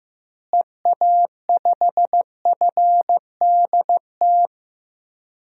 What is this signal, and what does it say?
Morse code, heard as a keyed tone, about 15 wpm, 700 Hz. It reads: EA5FDT
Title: DXCC-100 Call Signs - 6 Characters - Repeated 2x - 15wpm